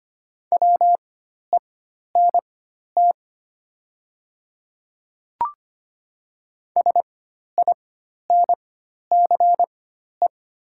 Morse code, heard as a keyed tone, about 25 wpm, spaced as for 12 wpm, 700 Hz. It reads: WENT SINCE
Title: 4x Character Spacing - Top 300 Words - 25wpm